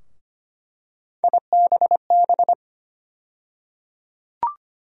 Morse code, heard as a keyed tone, about 25 wpm, 700 Hz. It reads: IBB
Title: USA Call Sign Postfixes - 3 Letters 25wpm